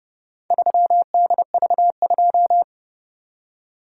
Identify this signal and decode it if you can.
Morse code, heard as a keyed tone, about 29 wpm, 700 Hz. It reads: 3DV2